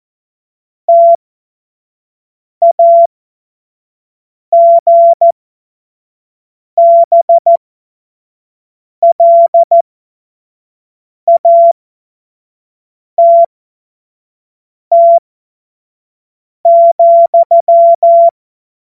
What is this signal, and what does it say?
Morse code, heard as a keyed tone, about 14 wpm, spaced as for 5 wpm, 680 Hz. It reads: TAGBLATT,
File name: Witz-16-September-2025-25-F70